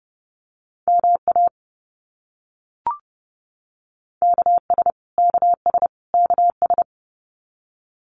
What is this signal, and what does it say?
Morse code, heard as a keyed tone, about 30 wpm, 700 Hz. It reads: MA KS KS KS
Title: US States Repeated 3x 30wpm